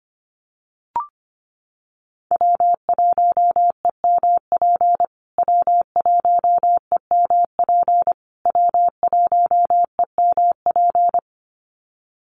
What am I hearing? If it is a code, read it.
Morse code, heard as a keyed tone, about 25 wpm, 700 Hz. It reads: W1EMP W1EMP W1EMP
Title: USA Call Signs Repeated 3x 25wpm